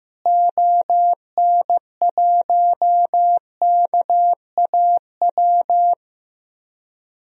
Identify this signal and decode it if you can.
Morse code, heard as a keyed tone, about 15 wpm, 700 Hz. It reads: ON1KAW
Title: DXCC-100 Call Signs - 6 Characters - Repeated 3x - 15wpm